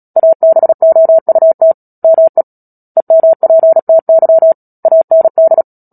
Morse code, heard as a keyed tone, about 37 wpm, 650 Hz. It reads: ABOUT ME EMPTY AND
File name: Book Two / 10-Chapter 10 - Epilogue - 36 WPM